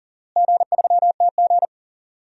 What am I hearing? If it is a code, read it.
Morse code, heard as a keyed tone, about 40 wpm, 700 Hz. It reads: G3TG
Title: DXCC-100 Call Signs - 4 Characters - Repeated 2x - 40wpm